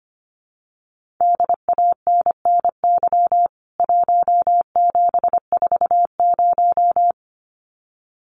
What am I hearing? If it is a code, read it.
Morse code, heard as a keyed tone, about 25 wpm, 700 Hz. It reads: DANNY 1740
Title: CWT Exchange 25wpm